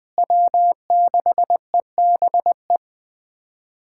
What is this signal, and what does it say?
Morse code, heard as a keyed tone, about 20 wpm, 700 Hz. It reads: W6EBE
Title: USA Callsign DE Callsign - 5 Characters - 20wpm